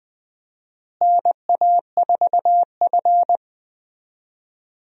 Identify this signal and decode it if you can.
Morse code, heard as a keyed tone, about 20 wpm, 700 Hz. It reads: NA4F
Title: USA Call Signs - 4 Characters - Repeated 2x - 20wpm